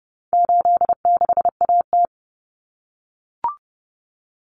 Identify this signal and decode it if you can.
Morse code, heard as a keyed tone, about 30 wpm, 700 Hz. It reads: 86AT
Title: Common Cut Numbers - 4-Digit 30wpm